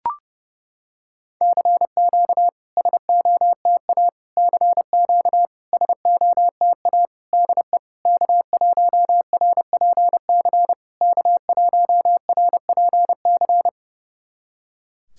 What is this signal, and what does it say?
Morse code, heard as a keyed tone, about 30 wpm, 700 Hz. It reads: CQ SOTA CQ SOTA DE K1RPC K1RPC
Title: SOTA QSO Practice 30wpm